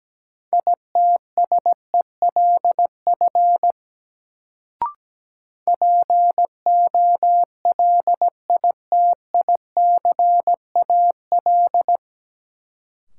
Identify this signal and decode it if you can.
Morse code, heard as a keyed tone, about 17 wpm, 700 Hz. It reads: ITSELF POLITICAL
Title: Sentences from Top 800 Words - New Word Drill - 17wpm